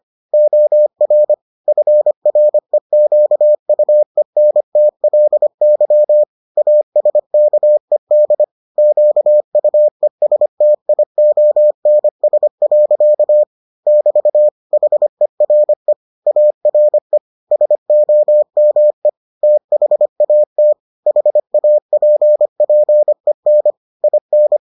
Morse code, heard as a keyed tone, about 25 wpm, 600 Hz. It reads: OR FREQUENTLY ASKED QUESTIONS. <BT> HERE ARE SOME THAT HAPPEN IN